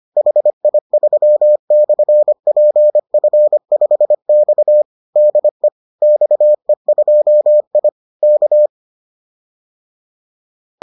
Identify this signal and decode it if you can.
Morse code, heard as a keyed tone, about 25 wpm, 600 Hz. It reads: HI3/PF5X DE XE2I K
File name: 25wpm-6